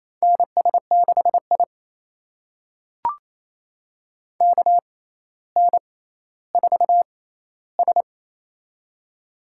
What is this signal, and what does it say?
Morse code, heard as a keyed tone, about 28 wpm, spaced as for 14 wpm, 700 Hz. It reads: NS6I KN4S
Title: USA Call Signs - 4 Characters - 6x Character Spacing - 28wpm